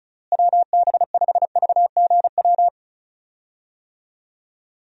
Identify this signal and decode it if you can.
Morse code, heard as a keyed tone, about 34 wpm, 700 Hz. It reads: WB5VGW
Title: USA Call Signs - 6 Characters - Repeated 3x - 35wpm